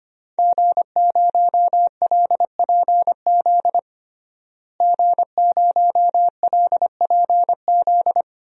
Morse code, heard as a keyed tone, about 25 wpm, 700 Hz. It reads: G0LPZ G0LPZ
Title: DXCC-100 Call Signs - 5 Characters - Repeated 3x - 25wpm